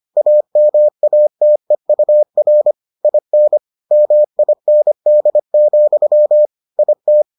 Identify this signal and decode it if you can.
Morse code, heard as a keyed tone, about 25 wpm, 600 Hz. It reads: AMATEUR IN MIND, IT